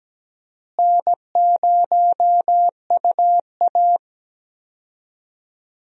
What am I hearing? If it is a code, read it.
Morse code, heard as a keyed tone, about 17 wpm, 700 Hz. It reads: N0UA
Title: USA Call Signs - 4 Characters - Repeated 3x - 17wpm